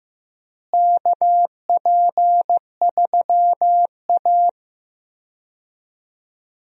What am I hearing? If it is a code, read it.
Morse code, heard as a keyed tone, about 15 wpm, 700 Hz. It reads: KP3A
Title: USA Call Signs - 4 Characters - Repeated 2x - 15wpm